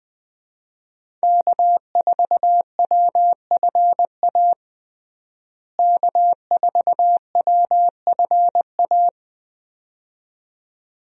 Morse code, heard as a keyed tone, about 20 wpm, 700 Hz. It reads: K4WFA K4WFA
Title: USA Call Signs - 5 Characters - Repeated 2x - 20wpm